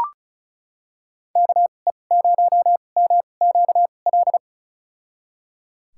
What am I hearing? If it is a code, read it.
Morse code, heard as a keyed tone, about 34 wpm, spaced as for 25 wpm, 700 Hz. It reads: KE0MQL
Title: USA Call Signs - 6 Characters - 2x Character Spacing - 35wpm